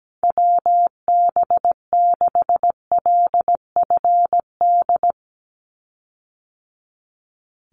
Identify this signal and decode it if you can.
Morse code, heard as a keyed tone, about 17 wpm, 700 Hz. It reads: WB6LFD
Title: USA Call Signs 17wpm